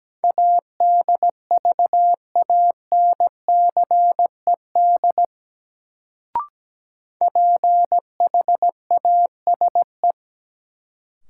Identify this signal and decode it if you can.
Morse code, heard as a keyed tone, about 17 wpm, 700 Hz. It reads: ADVANCED PHASE
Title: Sentences from Top 1200 Words - New Word Drill - 17wpm